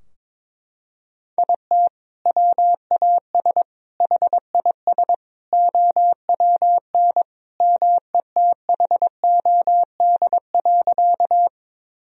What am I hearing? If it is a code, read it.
Morse code, heard as a keyed tone, about 22 wpm, 700 Hz. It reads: IT WAS HIS OWN METHOD.
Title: Sentences from Top 1000 Words - Review Part 3 - 22wpm